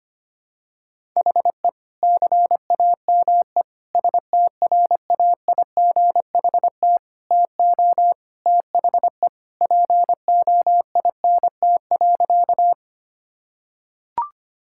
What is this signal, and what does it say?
Morse code, heard as a keyed tone, about 25 wpm, 700 Hz. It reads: HE CAME STRAIGHT TO THE POINT.
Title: Sentences from Top 1000 Words - Review Part 6 - 25wpm